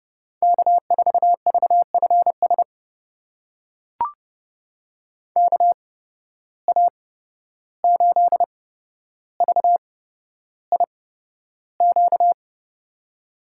Morse code, heard as a keyed tone, about 30 wpm, spaced as for 12 wpm, 700 Hz. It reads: K4VFS KA8VIQ
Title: USA Call Signs - 8x Character Spacing - 30wpm